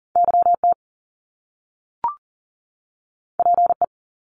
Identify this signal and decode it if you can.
Morse code, heard as a keyed tone, about 41 wpm, 700 Hz. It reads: YT PE
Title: Canadian Provinces and Territories 40wpm